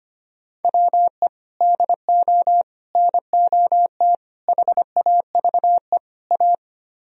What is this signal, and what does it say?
Morse code, heard as a keyed tone, about 33 wpm, 700 Hz. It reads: WE DO NOT HAVE A